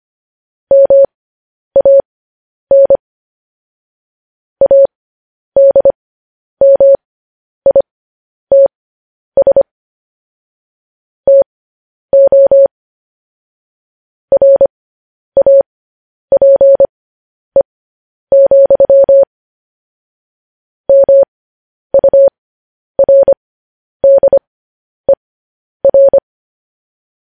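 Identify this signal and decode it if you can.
Morse code, heard as a keyed tone, about 25 wpm, spaced as for 10 wpm, 550 Hz. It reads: MAN ADMITS TO RAPE, MURDER
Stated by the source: Morse code news headlines practice transmission.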